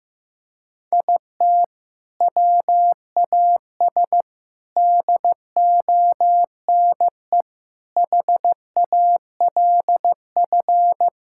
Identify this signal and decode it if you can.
Morse code, heard as a keyed tone, about 15 wpm, 700 Hz. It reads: IT WAS DONE HALF
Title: Sentences from Top 1000 Words - Review Part 5 - 15wpm